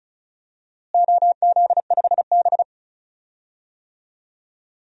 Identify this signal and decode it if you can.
Morse code, heard as a keyed tone, about 36 wpm, 700 Hz. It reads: OZ5B
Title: DXCC-100 Call Signs - 4 Characters - Repeated 3x - 35wpm